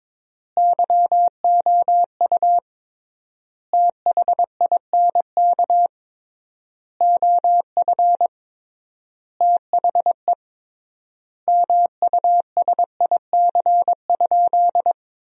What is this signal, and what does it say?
Morse code, heard as a keyed tone, about 22 wpm, 700 Hz. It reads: YOU THINK OF THE MUSIC?
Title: 3x Word-Spacing - Sentences from Top 300 Words - 22wpm